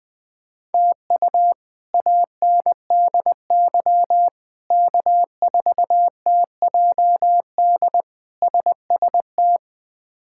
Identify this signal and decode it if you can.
Morse code, heard as a keyed tone, about 20 wpm, 700 Hz. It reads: TU ANDY K4TJD SST
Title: K1USN SST QSO Practice - 20wpm